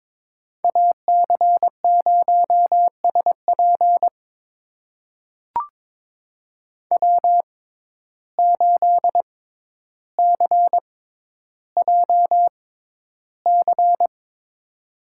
Morse code, heard as a keyed tone, about 22 wpm, spaced as for 11 wpm, 700 Hz. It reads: AC0SP W8CJC
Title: USA Call Signs - 5 Characters - 6x Character Spacing - 22wpm